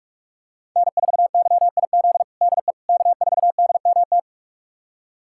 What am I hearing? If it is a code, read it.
Morse code, heard as a keyed tone, about 46 wpm, 700 Hz. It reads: N4YIZ DE K4DMT
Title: USA Callsign DE Callsign - 5 Characters - 45wpm